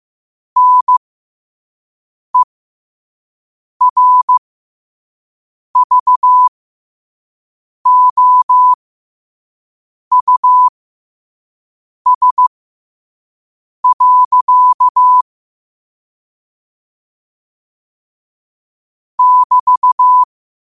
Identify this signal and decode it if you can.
Morse code, heard as a keyed tone, about 15 wpm, spaced as for 5 wpm, 1000 Hz. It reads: NERVOUS. <BT>
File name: qotdfiles / 5wpm